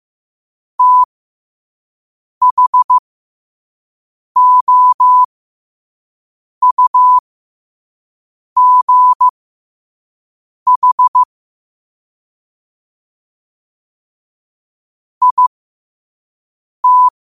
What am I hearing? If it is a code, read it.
Morse code, heard as a keyed tone, about 15 wpm, spaced as for 5 wpm, 1000 Hz. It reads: THOUGH IT
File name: qotdfiles / 5wpm